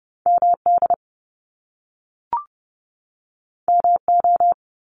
Morse code, heard as a keyed tone, about 30 wpm, 700 Hz. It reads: MD MO
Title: US States 30wpm